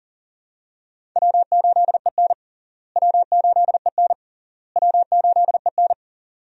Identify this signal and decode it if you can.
Morse code, heard as a keyed tone, about 40 wpm, 700 Hz. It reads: W8EN W8EN W8EN
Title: USA Call Signs - 4 Characters - Repeated 3x - 40wpm